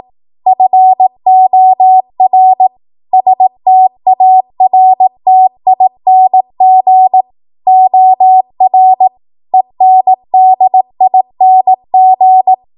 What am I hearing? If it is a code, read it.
Morse code, heard as a keyed tone, about 18 wpm, 750 Hz. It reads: FOR STARTING OR ENDING
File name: Archive / 18 WPM / 18WPM